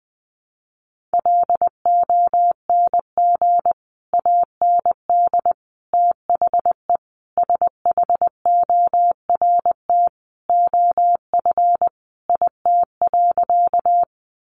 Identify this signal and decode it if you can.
Morse code, heard as a keyed tone, about 20 wpm, 700 Hz. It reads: LONG AND THE SHORT OF IT.
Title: Sentences from Top 400 Words 20wpm